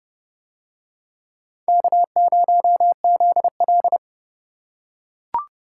Morse code, heard as a keyed tone, about 29 wpm, 700 Hz. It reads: K0ZL